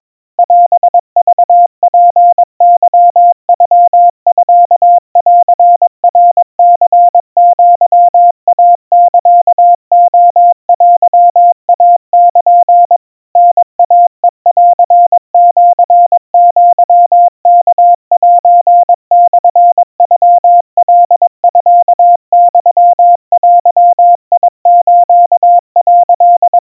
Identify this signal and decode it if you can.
Morse code (Japanese Wabun code), heard as a keyed tone, about 22 wpm, 700 Hz. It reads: オクツケノミンナニアイサレテイル タイヘンシアワセモノカミユデス」
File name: 80 cpm 700 Hz Crispy